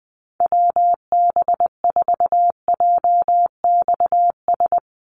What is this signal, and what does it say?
Morse code, heard as a keyed tone, about 20 wpm, 700 Hz. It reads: WB4JXS
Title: USA Call Signs 20wpm